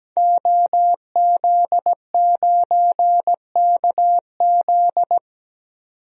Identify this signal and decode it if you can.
Morse code, heard as a keyed tone, about 17 wpm, 700 Hz. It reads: OZ9KZ
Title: DXCC-100 Call Signs - 5 Characters - 17wpm